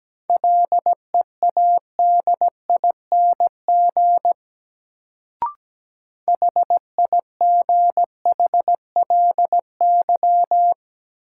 Morse code, heard as a keyed tone, about 17 wpm, 700 Hz. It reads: LEADING HIGHLY
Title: Sentences from Top 1100 Words - New Word Drill Alt - 17wpm